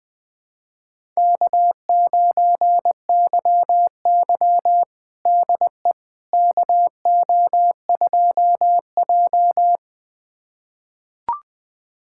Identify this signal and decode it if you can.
Morse code, heard as a keyed tone, about 21 wpm, 700 Hz. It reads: K9YY DE KO2J
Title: USA Callsign DE Callsign - 4 Characters - 20wpm